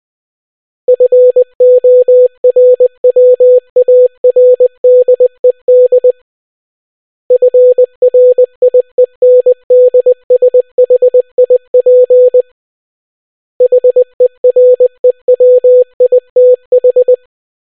Morse code, heard as a keyed tone, about 20 wpm, 500 Hz. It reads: FORWARDED FRIENDSHIP HEREWITH